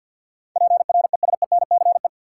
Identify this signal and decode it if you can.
Morse code, heard as a keyed tone, about 53 wpm, 700 Hz. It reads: PRESENCE
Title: Sets of 3 Words - New Word Drill - 50wpm